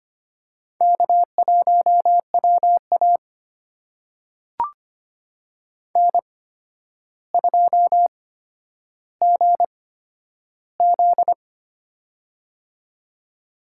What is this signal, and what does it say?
Morse code, heard as a keyed tone, about 25 wpm, spaced as for 11 wpm, 700 Hz. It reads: K1WA N2GZ
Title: USA Call Signs - 4 Characters - 8x Character Spacing - 25wpm